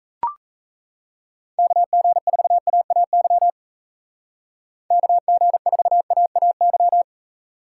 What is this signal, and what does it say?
Morse code, heard as a keyed tone, about 34 wpm, 700 Hz. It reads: KG4AAY KG4AAY
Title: Speed Racing - USA 2x3 Call Signs 28wpm